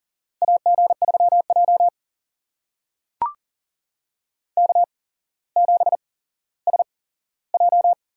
Morse code, heard as a keyed tone, about 43 wpm, spaced as for 17 wpm, 700 Hz. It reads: AG3J K7SJ
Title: USA Call Signs - 4 Characters - 8x Character Spacing - 40wpm